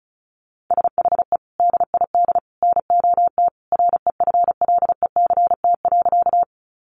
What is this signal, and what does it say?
Morse code, heard as a keyed tone, about 34 wpm, 700 Hz. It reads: SHE DID NOT REFLECT.
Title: Sentences from Top 1500 Words 35wpm